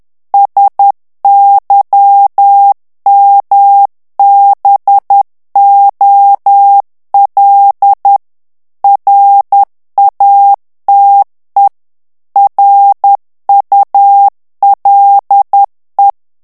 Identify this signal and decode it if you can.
Morse code, heard as a keyed tone, about 10 wpm, 800 Hz. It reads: SYMBOL RATE RULE
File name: ANGLAIS texte clair 3 vitesse 10 mots minute